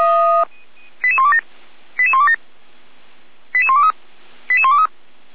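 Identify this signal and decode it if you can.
Five-tone selective call (ZVEI), heard as quick runs of five tones, tones of about 70 ms. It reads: 88127 88127 88123 88123
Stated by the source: Received by SDR; ZVEI five-tone code 88191.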